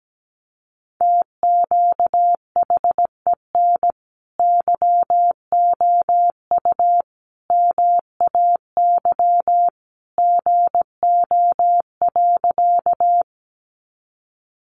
Morse code, heard as a keyed tone, about 17 wpm, 700 Hz. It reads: TQHEN YOU MAY GO.
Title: Sentences from Top 100 Words 17wpm